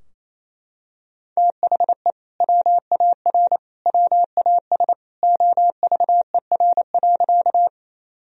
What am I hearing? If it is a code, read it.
Morse code, heard as a keyed tone, about 28 wpm, 700 Hz. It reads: THE WAR WAS OVER.
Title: Sentences from Top 1000 Words - Review Part 2 - 28wpm